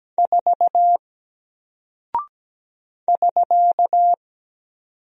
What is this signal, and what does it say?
Morse code, heard as a keyed tone, about 17 wpm, 700 Hz. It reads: 4 <SK>
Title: Single Character (Letter:Number:Punctuation:Prosign) 17wpm